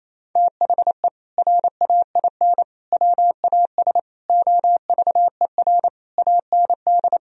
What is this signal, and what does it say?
Morse code, heard as a keyed tone, about 28 wpm, 700 Hz. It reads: THE RAIN WAS OVER AND
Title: Sentences from Top 1000 Words - Review Part 1 - 28wpm